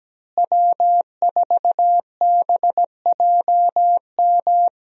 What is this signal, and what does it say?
Morse code, heard as a keyed tone, about 17 wpm, 700 Hz. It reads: W4BJM
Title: USA Call Signs - 5 Characters - Repeated 3x - 17wpm